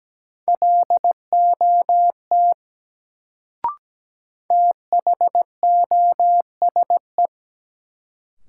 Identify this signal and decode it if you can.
Morse code, heard as a keyed tone, about 17 wpm, 700 Hz. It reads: LOT THOSE
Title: Sets of 2 Words - New Word Drill - 17wpm